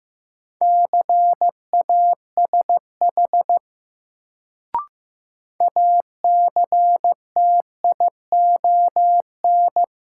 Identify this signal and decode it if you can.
Morse code, heard as a keyed tone, about 15 wpm, 700 Hz. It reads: CASH ACTION
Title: Sentences from Top 700 Words - New Word Drill - 15wpm